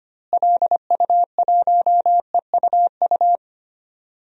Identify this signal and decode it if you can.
Morse code, heard as a keyed tone, about 25 wpm, 700 Hz. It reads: LU1EUU
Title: DXCC-100 Call Signs - 6 Characters - Repeated 2x - 25wpm